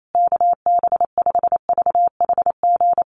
Morse code, heard as a keyed tone, about 28 wpm, 700 Hz. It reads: KB5VHG
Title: USA Call Signs Repeated 3x 28wpm